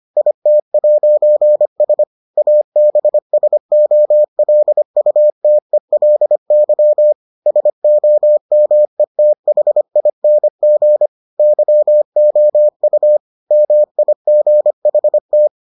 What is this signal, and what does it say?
Morse code, heard as a keyed tone, about 25 wpm, 600 Hz. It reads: IT'S ABSOLUTELY SOMETHING YOU MIGHT